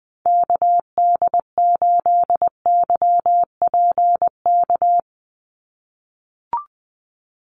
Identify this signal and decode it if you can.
Morse code, heard as a keyed tone, about 20 wpm, 700 Hz. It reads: KD8YPK
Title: USA Call Signs 20wpm